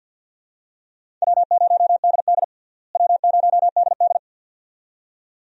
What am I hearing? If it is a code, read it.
Morse code, heard as a keyed tone, about 51 wpm, 700 Hz. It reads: W0DD W0DD
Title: USA Call Signs - 4 Characters - Repeated 2x - 50wpm